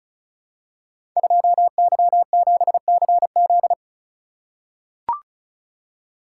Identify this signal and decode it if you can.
Morse code, heard as a keyed tone, about 36 wpm, 700 Hz. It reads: 2Y7CZ
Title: Code Groups - 5 Letter:Numbers - 35wpm